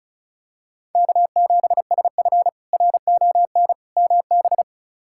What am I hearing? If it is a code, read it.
Morse code, heard as a keyed tone, about 36 wpm, 700 Hz. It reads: K7SF RON MB
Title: K1USN SST QSO Practice - 35wpm